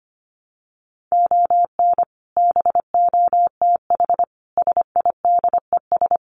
Morse code, heard as a keyed tone, about 25 wpm, 700 Hz. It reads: ON BOTH SIDES
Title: Sets of 5 Words 25wpm